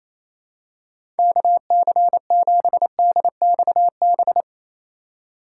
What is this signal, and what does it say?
Morse code, heard as a keyed tone, about 28 wpm, 700 Hz. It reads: KC7DXB
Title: USA Call Signs - 6 Characters - 2x Character Spacing - 28wpm